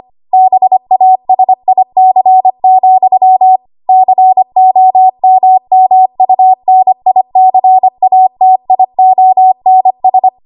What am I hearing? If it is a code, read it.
Morse code, heard as a keyed tone, about 25 wpm, 750 Hz. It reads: BASIC, COMMUNICATIONS
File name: Archive / 25 WPM / 25WPM